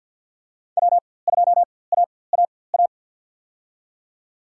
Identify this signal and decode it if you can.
Morse code, heard as a keyed tone, about 53 wpm, spaced as for 24 wpm, 700 Hz. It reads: W2AAA
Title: USA Call Signs - 5 Characters - 4x Character Spacing - 50wpm